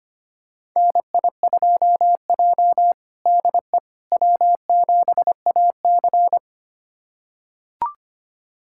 Morse code, heard as a keyed tone, about 25 wpm, 700 Hz. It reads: NI2J DE W7AC
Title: USA Callsign DE Callsign - 4 Characters - 25wpm